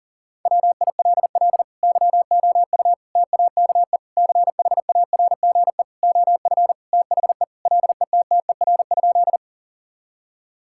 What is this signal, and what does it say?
Morse code, heard as a keyed tone, about 37 wpm, 700 Hz. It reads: WILL YOU TAKE CHARGE OF THE LETTER?
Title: Sentences from Top 1000 Words - Review Part 6 - 40wpm